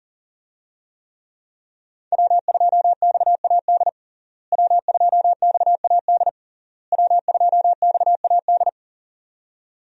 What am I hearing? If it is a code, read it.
Morse code, heard as a keyed tone, about 40 wpm, 700 Hz. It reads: W2XAD W2XAD W2XAD
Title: USA Call Signs - 5 Characters - Repeated 3x - 40wpm